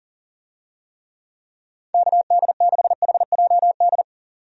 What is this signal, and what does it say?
Morse code, heard as a keyed tone, about 40 wpm, 700 Hz. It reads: KD6HJD
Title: USA Call Signs - 6 Characters - 40wpm